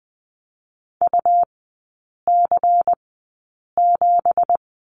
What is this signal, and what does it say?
Morse code, heard as a keyed tone, about 20 wpm, spaced as for 8 wpm, 700 Hz. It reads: UC7
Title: 191 Lesson 31e - Letter v - Three Characters 20wpm